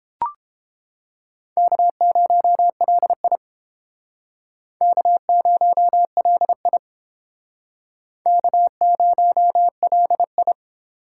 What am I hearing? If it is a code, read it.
Morse code, heard as a keyed tone, about 31 wpm, 700 Hz. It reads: K0LI K0LI K0LI